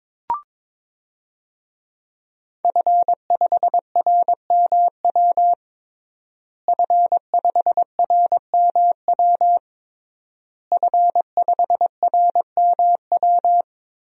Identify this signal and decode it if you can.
Morse code, heard as a keyed tone, about 22 wpm, 700 Hz. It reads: F5RMW F5RMW F5RMW
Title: DXCC-100 Call Signs - 5 Characters - Repeated 3x - 22wpm